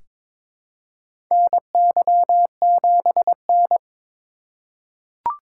Morse code, heard as a keyed tone, about 22 wpm, 700 Hz. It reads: NY7N
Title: K1USN SST QSO Practice - 22wpm